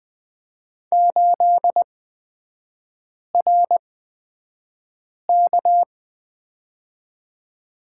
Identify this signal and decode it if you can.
Morse code, heard as a keyed tone, about 20 wpm, spaced as for 5 wpm, 690 Hz. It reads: 8RK
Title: 229 Lesson 37e - Letter x - Three Characters 20wpm